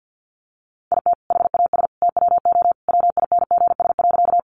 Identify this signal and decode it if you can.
Morse code, heard as a keyed tone, about 50 wpm, 700 Hz. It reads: IT HAS TWO WINGS.
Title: Additional Easy Phrases 50wpm